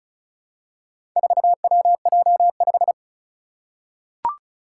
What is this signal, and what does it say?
Morse code, heard as a keyed tone, about 36 wpm, 700 Hz. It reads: 4WJ5
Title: Four Challenging Letter-Number Combinations 35wpm